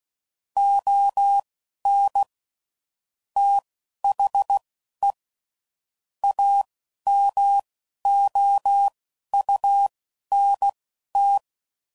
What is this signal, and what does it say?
Morse code, heard as a keyed tone, about 16 wpm, spaced as for 11 wpm, 800 Hz. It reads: ON THE AMOUNT
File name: CW Code Groups @10WPM / Technical Subjects / XL